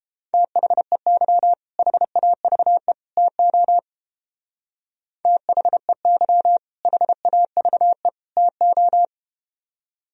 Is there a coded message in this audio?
Yes: Morse code from 0.3 to 9.1 s, heard as a keyed tone, about 30 wpm, 700 Hz. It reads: THEY HAVE TO THEY HAVE TO